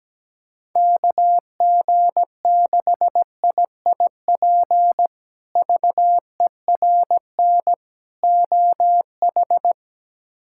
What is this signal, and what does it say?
Morse code, heard as a keyed tone, about 17 wpm, 700 Hz. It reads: KG6IIP VERN OH
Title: K1USN SST QSO Practice - 17wpm